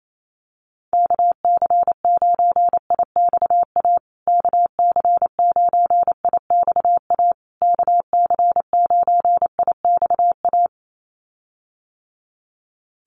Morse code, heard as a keyed tone, about 28 wpm, 700 Hz. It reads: KC9IXA KC9IXA KC9IXA
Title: USA Call Signs Repeated 3x 28wpm